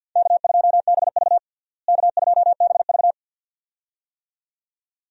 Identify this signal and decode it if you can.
Morse code, heard as a keyed tone, about 47 wpm, 700 Hz. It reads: K2BV K2BV
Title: USA Call Signs - 4 Characters - Repeated 2x - 50wpm